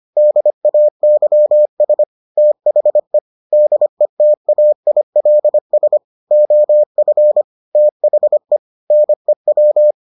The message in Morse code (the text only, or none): DAYS THE DETAILS OF THE NEW